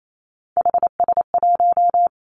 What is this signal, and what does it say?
Morse code, heard as a keyed tone, about 28 wpm, 700 Hz. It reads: HS1
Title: Three Challenging Letter-Number Combinations 28wpm